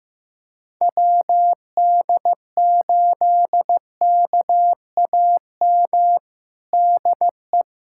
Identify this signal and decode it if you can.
Morse code, heard as a keyed tone, about 15 wpm, 700 Hz. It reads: WD8KAM DE
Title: USA Callsign DE Callsign 15wpm